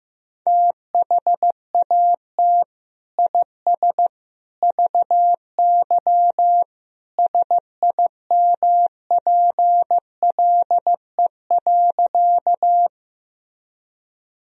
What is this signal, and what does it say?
Morse code, heard as a keyed tone, about 15 wpm, 700 Hz. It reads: THAT IS VY SIMPLE.
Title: Sentences from Top 400 Words and Abbreviations 15wpm